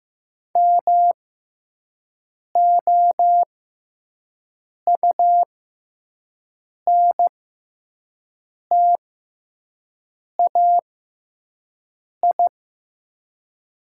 Morse code, heard as a keyed tone, about 15 wpm, spaced as for 5 wpm, 700 Hz. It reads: MOUNTAI
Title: 6x Character Spacing - Top 500 Words - 15wpm